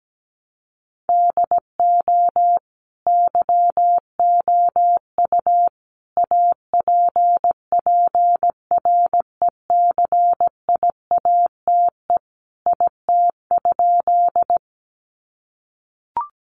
Morse code, heard as a keyed tone, about 17 wpm, 700 Hz. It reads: DO YOU APPRECIATE IT?
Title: Sentences from Top 1300 Words 17wpm